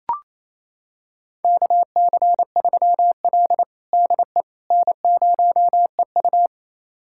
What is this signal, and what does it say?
Morse code, heard as a keyed tone, about 28 wpm, 700 Hz. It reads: KC3L DE N0EU
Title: USA Callsign DE Callsign - 4 Characters - 28wpm